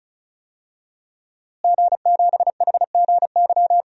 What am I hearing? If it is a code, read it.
Morse code, heard as a keyed tone, about 35 wpm, 700 Hz. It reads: G7HGY
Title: DXCC-100 Call Signs - 5 Characters - Repeated 2x - 35wpm